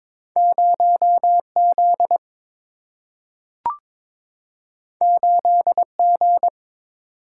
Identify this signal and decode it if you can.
Morse code, heard as a keyed tone, about 22 wpm, 700 Hz. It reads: 0 Z 8 G
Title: Single-Letter Single-Number 22wpm